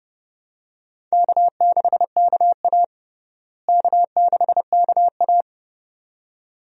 Morse code, heard as a keyed tone, about 30 wpm, 700 Hz. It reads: K6KA K6KA
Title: USA Call Signs - 4 Characters - Repeated 2x - 30wpm